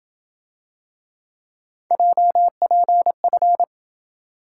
Morse code, heard as a keyed tone, about 27 wpm, 700 Hz. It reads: JPF